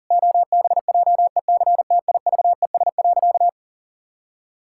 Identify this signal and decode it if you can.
Morse code, heard as a keyed tone, about 40 wpm, 700 Hz. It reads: OBJECTIVES.
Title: Pangram Challenge v2 - 40wpm